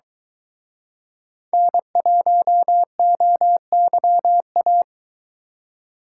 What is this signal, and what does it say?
Morse code, heard as a keyed tone, about 24 wpm, 700 Hz. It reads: N1OYA